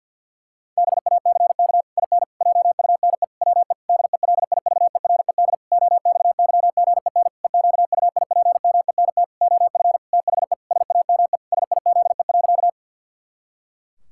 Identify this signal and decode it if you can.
Morse code, heard as a keyed tone, about 52 wpm, 700 Hz. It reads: BACK IN JUNE WE DELIVERED OXYGEN EQUIPMENT OF THE SAME SIZE.
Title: Pangram Challenge v2 - 50wpm